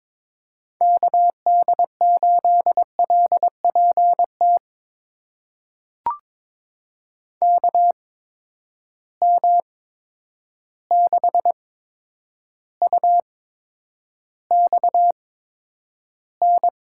Morse code, heard as a keyed tone, about 22 wpm, spaced as for 9 wpm, 700 Hz. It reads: KD8LPT KM6UXN
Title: USA Call Signs - 6 Characters - 8x Character Spacing - 22wpm